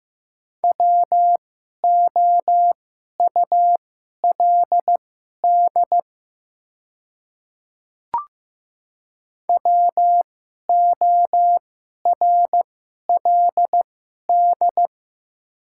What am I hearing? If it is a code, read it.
Morse code, heard as a keyed tone, about 15 wpm, spaced as for 11 wpm, 700 Hz. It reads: WOULD WORLD
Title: 2x Character Spacing - Top 200 Words - 15wpm